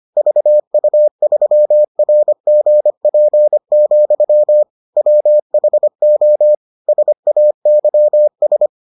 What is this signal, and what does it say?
Morse code, heard as a keyed tone, about 25 wpm, 600 Hz. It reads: VU3RGP, WHO SAYS